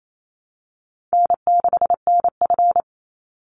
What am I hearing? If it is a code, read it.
Morse code, heard as a keyed tone, about 28 wpm, 700 Hz. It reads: N6NF
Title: Easy Call Signs - 28wpm